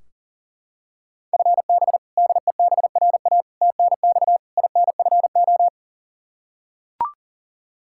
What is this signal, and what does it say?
Morse code, heard as a keyed tone, about 40 wpm, 700 Hz. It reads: FB DEBRA TNX INFO
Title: Practice QSO - Exchange 3 - More Info - 40wpm